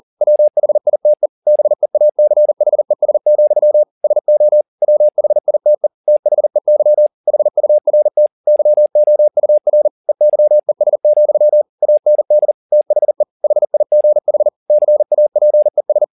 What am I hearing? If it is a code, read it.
Morse code, heard as a keyed tone, about 40 wpm, 600 Hz. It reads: WHITE BEACHES, SO WHITE THEY HURT YOUR EYES, AND THE HIGH CAPES